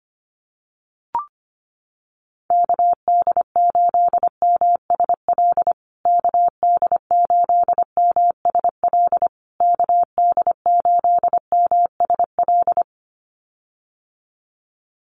Morse code, heard as a keyed tone, about 25 wpm, 700 Hz. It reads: KD8MSL KD8MSL KD8MSL
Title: USA Call Signs Repeated 3x 25wpm